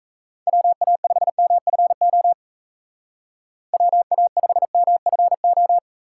Morse code, heard as a keyed tone, about 33 wpm, 700 Hz. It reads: WA5MFO WA5MFO
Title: Speed Racing - USA 2x3 Call Signs 28wpm